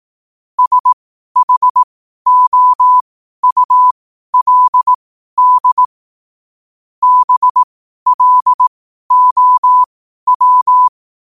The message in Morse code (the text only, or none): SHOULD BLOW